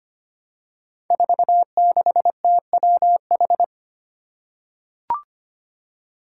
Morse code, heard as a keyed tone, about 25 wpm, 700 Hz. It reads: 46TWH
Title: Five Challenging Letter-Number Combinations 25wpm